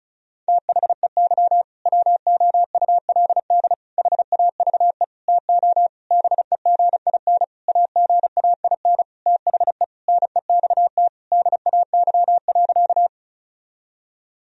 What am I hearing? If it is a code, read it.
Morse code, heard as a keyed tone, about 36 wpm, 700 Hz. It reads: THEY WOULD HAVE TO BEGIN AGAIN THE NEXT DAY.
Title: Sentences from Top 1000 Words - Review Part 1 - 35wpm